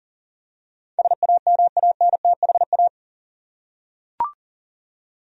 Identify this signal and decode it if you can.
Morse code, heard as a keyed tone, about 38 wpm, 700 Hz. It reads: SAMANTHA
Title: USA Common Names - 40wpm